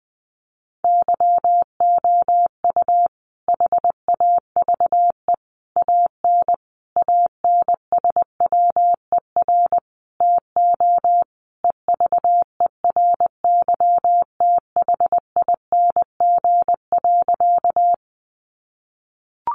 Morse code, heard as a keyed tone, about 20 wpm, 700 Hz. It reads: YOU HAVE AN ANSWER TO EVERYTHING.
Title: Sentences from Top 700 Words 20wpm